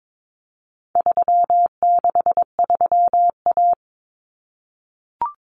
Morse code, heard as a keyed tone, about 22 wpm, 700 Hz. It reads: 363A
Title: Common Cut Numbers - 4-Digit 22wpm